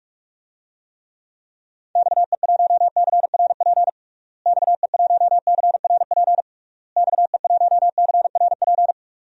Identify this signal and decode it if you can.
Morse code, heard as a keyed tone, about 46 wpm, 700 Hz. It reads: XE1CRP XE1CRP XE1CRP
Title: DXCC-100 Call Signs - 6 Characters - Repeated 3x - 45wpm